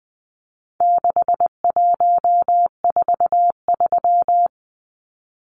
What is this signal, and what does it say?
Morse code, heard as a keyed tone, about 20 wpm, 700 Hz. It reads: 6143
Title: Numbers - 4-Digit 20wpm